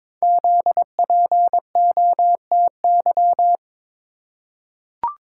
Morse code, heard as a keyed tone, about 22 wpm, 700 Hz. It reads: ZPOTY
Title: Code Groups - 5 Letters - 22wpm